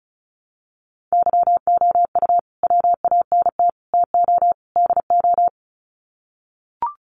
Morse code, heard as a keyed tone, about 34 wpm, 700 Hz. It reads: YOU WANT TO DO
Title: Sets of 4 Words 35wpm